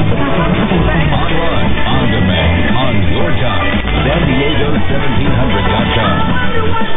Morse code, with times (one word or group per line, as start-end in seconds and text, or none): none